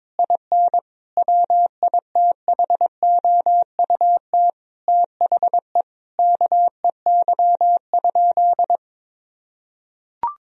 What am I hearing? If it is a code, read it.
Morse code, heard as a keyed tone, about 21 wpm, 700 Hz. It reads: IN WITHOUT THE KEY?
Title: Sentences from Top 1000 Words - Review Part 4 - 22wpm